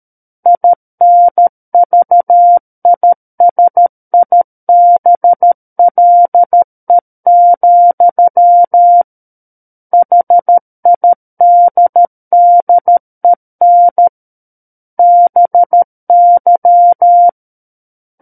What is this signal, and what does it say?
Morse code, heard as a keyed tone, about 13 wpm, 700 Hz. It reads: INVISIBLE, HIDDEN BY
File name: Book One / 04-Chapter 4 - 13 WPM